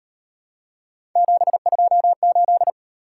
Morse code, heard as a keyed tone, about 38 wpm, 700 Hz. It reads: 728